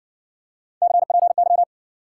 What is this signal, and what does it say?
Morse code, heard as a keyed tone, about 54 wpm, 700 Hz. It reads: CPY